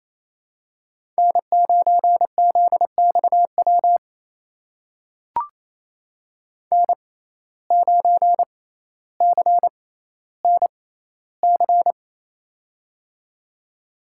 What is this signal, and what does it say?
Morse code, heard as a keyed tone, about 28 wpm, spaced as for 14 wpm, 700 Hz. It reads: N9ZXW N9CNC
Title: USA Call Signs - 5 Characters - 6x Character Spacing - 28wpm